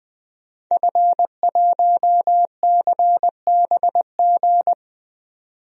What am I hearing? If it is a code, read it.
Morse code, heard as a keyed tone, about 20 wpm, 700 Hz. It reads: F1CBG
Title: DXCC-100 Call Signs - 5 Characters - 20wpm